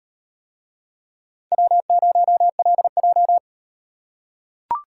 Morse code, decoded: W0LJ

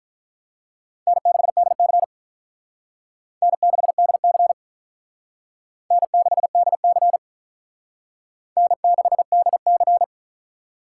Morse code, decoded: N6DC N6DC N6DC N6DC

